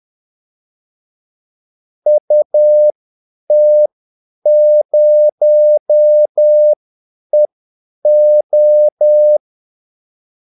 Morse code, heard as a keyed tone, about 10 wpm, spaced as for 8 wpm, 600 Hz. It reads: UT0EO
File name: Callsigns-10-8wpm